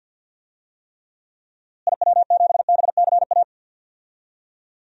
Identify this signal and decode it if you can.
Morse code, heard as a keyed tone, about 53 wpm, 700 Hz. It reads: IW7BCA
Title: DXCC-100 Call Signs - 6 Characters - Repeated 2x - 50wpm